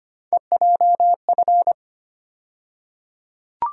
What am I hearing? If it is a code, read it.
Morse code, heard as a keyed tone, about 25 wpm, 700 Hz. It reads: EJF
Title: USA Call Sign Postfixes - 3 Letters 25wpm